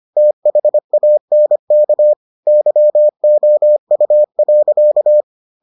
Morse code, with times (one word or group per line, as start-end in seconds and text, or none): 0.2-2.1 THANK
2.5-5.2 YOU.